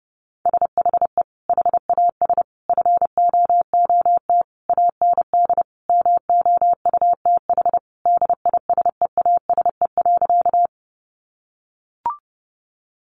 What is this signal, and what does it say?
Morse code, heard as a keyed tone, about 30 wpm, 700 Hz. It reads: SHE HAS FOOT AND MOUTH DISEASE.
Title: Sentences from Top 1500 Words 30wpm